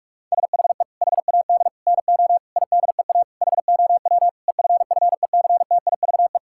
Morse code, heard as a keyed tone, about 46 wpm, 700 Hz. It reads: SHE HAD NO IDEA HOW EFFECTIVE
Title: Sentences from Top 1000 Words - Review Part 6 - 45wpm